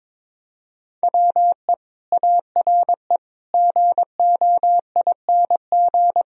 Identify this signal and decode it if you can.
Morse code, heard as a keyed tone, about 24 wpm, 700 Hz. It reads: WE ARE GOING
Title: Speed-Racing - Sets of 4 Words 22wpm